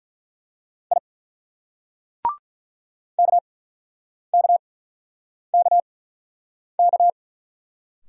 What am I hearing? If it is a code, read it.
Morse code, heard as a keyed tone, about 50 wpm, spaced as for 9 wpm, 700 Hz. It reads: I KKKK